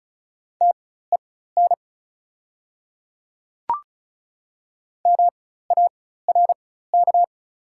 Morse code, heard as a keyed tone, about 34 wpm, spaced as for 16 wpm, 700 Hz. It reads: TEN MARK
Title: 4x Character Spacing - Top 500 Words - 35wpm